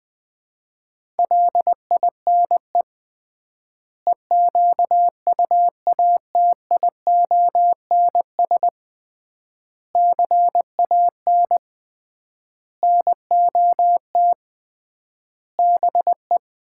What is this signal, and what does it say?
Morse code, heard as a keyed tone, about 20 wpm, 700 Hz. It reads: LINE EQUATIONS CAN NOT BE
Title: 3x Word-Spacing - Sets of 5 Words - 20wpm